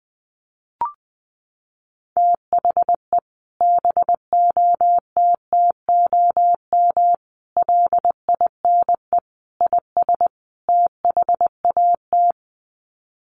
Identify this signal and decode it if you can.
Morse code, heard as a keyed tone, about 20 wpm, 700 Hz. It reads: THE BOTTOM LINE IS THAT
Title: Sets of 5 Words Encore 20wpm